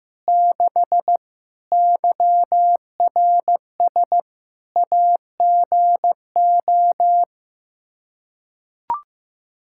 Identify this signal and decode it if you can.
Morse code, heard as a keyed tone, about 16 wpm, 700 Hz. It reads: 6 YRS AGO
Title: Practice QSO - Exchange 3 - More Info - 15wpm